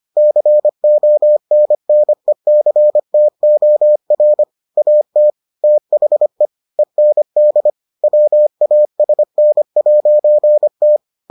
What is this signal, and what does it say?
Morse code, heard as a keyed tone, about 25 wpm, 600 Hz. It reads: CONNECTOR AT THE END WASN'T